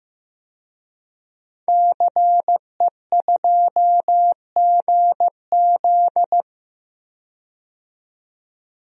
Morse code, heard as a keyed tone, about 15 wpm, 700 Hz. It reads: CE2GZ
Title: DXCC-100 Call Signs - 5 Characters - 15wpm